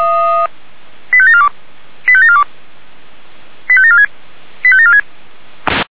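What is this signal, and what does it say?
Five-tone selective call (ZVEI), heard as quick runs of five tones, tones of about 70 ms. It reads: 85742 85742 85746 85746
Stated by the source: ZVEI five-tone selective call, code 85719, received by SDR.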